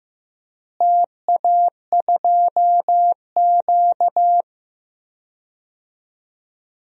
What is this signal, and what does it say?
Morse code, heard as a keyed tone, about 15 wpm, 700 Hz. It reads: TA2Q
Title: DXCC-100 Call Signs - 4 Characters - Repeated 2x - 15wpm